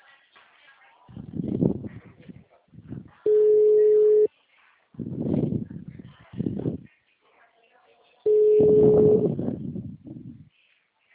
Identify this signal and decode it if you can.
Ringback tone (Europe, 425 Hz), with speech.
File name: out- -2013- - - .1136